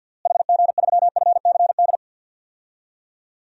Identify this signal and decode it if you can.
Morse code, heard as a keyed tone, about 53 wpm, 700 Hz. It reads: HG3FCD
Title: DXCC-100 Call Signs - 6 Characters - Repeated 2x - 50wpm